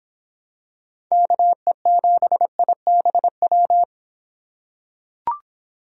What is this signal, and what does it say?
Morse code, heard as a keyed tone, about 26 wpm, 700 Hz. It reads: KE7IBW